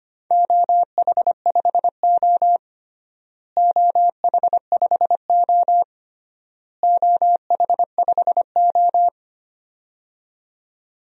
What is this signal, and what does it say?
Morse code, heard as a keyed tone, about 25 wpm, 700 Hz. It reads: OH5O OH5O OH5O
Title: DXCC-100 Call Signs - 4 Characters - Repeated 3x - 25wpm